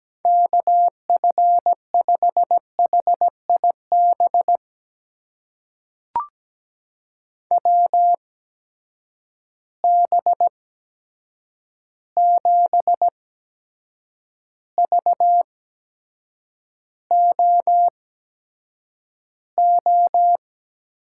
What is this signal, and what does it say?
Morse code, heard as a keyed tone, about 17 wpm, spaced as for 7 wpm, 700 Hz. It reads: KF5HIB WB7VOO
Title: USA Call Signs - 6 Characters - 8x Character Spacing - 17wpm